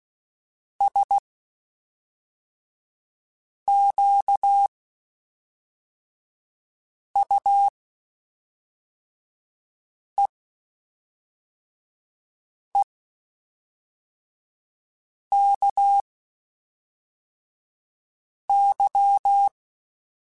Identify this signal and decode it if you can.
Morse code, heard as a keyed tone, about 16 wpm, spaced as for 3 wpm, 800 Hz. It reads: SQUEEKY
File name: Spaced / Chuckles Spaced-mp3s / Squeaky sp @5